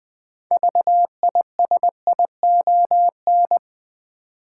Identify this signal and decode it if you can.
Morse code, heard as a keyed tone, about 20 wpm, 700 Hz. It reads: VISION
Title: 6 Letter Words - First 3 Letters Given - 20wpm